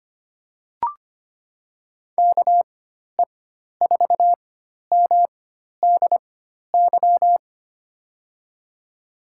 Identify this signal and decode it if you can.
Morse code, heard as a keyed tone, about 25 wpm, spaced as for 12 wpm, 700 Hz. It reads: KE4MDY
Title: USA Call Signs - 6 Characters - 4x Character Spacing - 25wpm